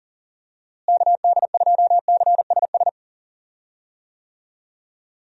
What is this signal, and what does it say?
Morse code, heard as a keyed tone, about 40 wpm, 700 Hz. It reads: KD2CSS
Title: USA Call Signs - 6 Characters - Repeated 2x - 40wpm